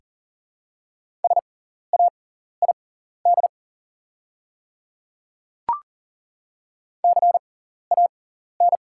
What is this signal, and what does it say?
Morse code, heard as a keyed tone, about 40 wpm, spaced as for 14 wpm, 700 Hz. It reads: SAID CAN
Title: 6x Character Spacing - Top 100 Words - 40wpm